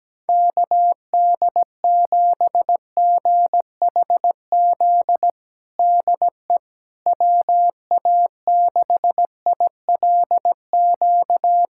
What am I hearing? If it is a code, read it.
Morse code, heard as a keyed tone, about 17 wpm, 700 Hz. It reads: KD7GHZ DE WA6ILQ
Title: USA Callsign DE Callsign - 6 Characters - 17wpm